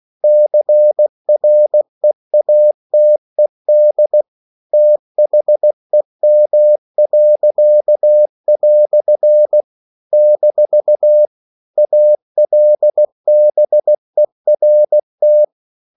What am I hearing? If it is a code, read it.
Morse code, heard as a keyed tone, about 16 wpm, 600 Hz. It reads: CREATED THEM." - ALBERT
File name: CW / Practice / EnglishQuotations0216